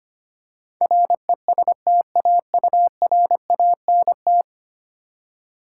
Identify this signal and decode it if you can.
Morse code, heard as a keyed tone, about 25 wpm, 700 Hz. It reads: RESTAURANT
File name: Ten Letter Words - 25wpm